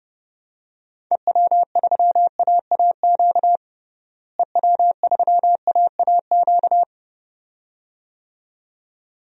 Morse code, heard as a keyed tone, about 30 wpm, 700 Hz. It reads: EW3AAQ EW3AAQ
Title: DXCC-100 Call Signs - 6 Characters - Repeated 2x - 30wpm